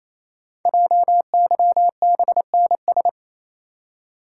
Morse code, heard as a keyed tone, about 27 wpm, 700 Hz. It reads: JYBNS
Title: Code Groups - 5 Letters - 28wpm